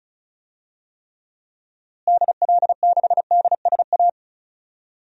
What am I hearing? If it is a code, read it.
Morse code, heard as a keyed tone, about 34 wpm, 700 Hz. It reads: DL6DSA
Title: DXCC-100 Call Signs - 6 Characters - Repeated 2x - 35wpm